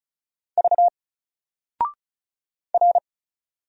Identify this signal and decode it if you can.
Morse code, heard as a keyed tone, about 36 wpm, 700 Hz. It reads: V R
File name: Single-Letter-Number-35wpm